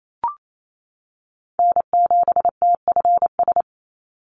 Morse code, heard as a keyed tone, about 28 wpm, 700 Hz. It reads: N7TFS
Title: USA Call Signs 28wpm